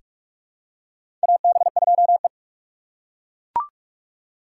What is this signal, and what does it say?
Morse code, heard as a keyed tone, about 42 wpm, 700 Hz. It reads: AB2E
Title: USA Call Signs - 4 Characters - 6x Character Spacing - 45wpm